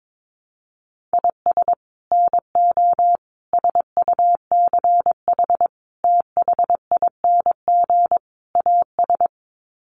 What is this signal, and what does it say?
Morse code, heard as a keyed tone, about 22 wpm, 700 Hz. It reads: IS NO SUCH THING AS
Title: Sets of 5 Words Encore 22wpm